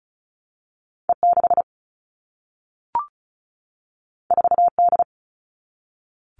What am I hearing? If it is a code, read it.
Morse code, heard as a keyed tone, about 36 wpm, 700 Hz. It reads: E6 4D
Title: All Cut Numbers - 2-Digit 35wpm